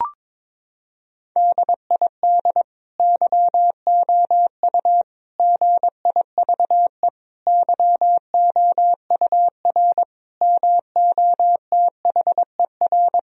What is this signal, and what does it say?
Morse code, heard as a keyed tone, about 22 wpm, 700 Hz. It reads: DID YOU GIVE YOUR MOTHER
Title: Sentences from Top 1000 Words - Review Part 4 - 22wpm